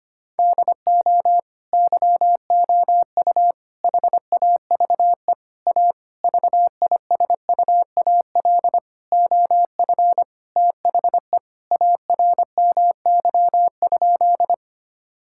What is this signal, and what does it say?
Morse code, heard as a keyed tone, about 25 wpm, 700 Hz. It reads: DO YOU HAVE A VISUAL OF THE ARMY?
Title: Sentences from Top 1000 Words - Review Part 6 - 25wpm